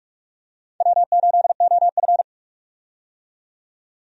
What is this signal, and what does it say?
Morse code, heard as a keyed tone, about 46 wpm, 700 Hz. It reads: W8OF
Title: USA Call Signs - 4 Characters - 45wpm